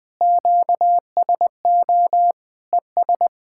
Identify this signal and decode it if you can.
Morse code, heard as a keyed tone, about 21 wpm, 700 Hz. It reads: QSO ES
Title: Speed-Racing Common QSO Phrases Encore - 20wpm